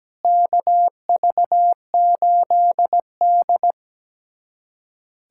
Morse code, heard as a keyed tone, about 17 wpm, 700 Hz. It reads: KV8D
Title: USA Callsign DE Callsign - 4 Characters - 17wpm